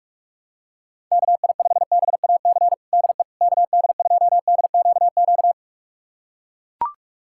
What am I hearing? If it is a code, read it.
Morse code, heard as a keyed tone, about 41 wpm, 700 Hz. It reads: KI5BAC DE KD2DQQ
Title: USA Callsign DE Callsign - 6 Characters - 45wpm